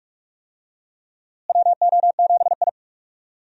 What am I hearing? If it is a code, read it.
Morse code, heard as a keyed tone, about 45 wpm, 700 Hz. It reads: WO7I
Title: USA Call Signs - 4 Characters - Repeated 3x - 45wpm